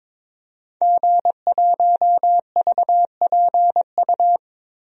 Morse code, heard as a keyed tone, about 22 wpm, 700 Hz. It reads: G1VPU
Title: DXCC-100 Call Signs - 5 Characters - Repeated 3x - 22wpm